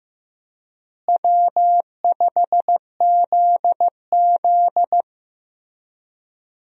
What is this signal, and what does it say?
Morse code, heard as a keyed tone, about 16 wpm, 700 Hz. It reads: W5ZZ
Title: USA Call Signs - 4 Characters - Repeated 3x - 15wpm